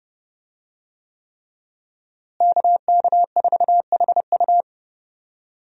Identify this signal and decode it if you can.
Morse code, heard as a keyed tone, about 30 wpm, 700 Hz. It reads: KK4HU
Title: USA Call Signs - 5 Characters - Repeated 3x - 30wpm